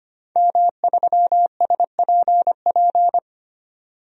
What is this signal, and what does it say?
Morse code, heard as a keyed tone, about 25 wpm, 700 Hz. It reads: M3SPP
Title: DXCC-100 Call Signs - 5 Characters - 25wpm